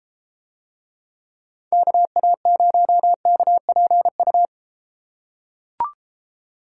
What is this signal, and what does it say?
Morse code, heard as a keyed tone, about 31 wpm, 700 Hz. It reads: KA0KPU